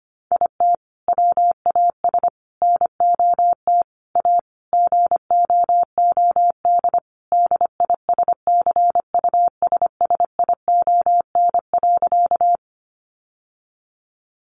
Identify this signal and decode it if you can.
Morse code, heard as a keyed tone, about 25 wpm, 700 Hz. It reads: IT WAS NOT A GOOD DISCUSSION.
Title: Sentences from Top 1100 Words 25wpm